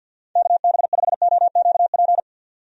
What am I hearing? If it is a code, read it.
Morse code, heard as a keyed tone, about 47 wpm, 700 Hz. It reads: KB5OQP
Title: USA Call Signs - 6 Characters - 50wpm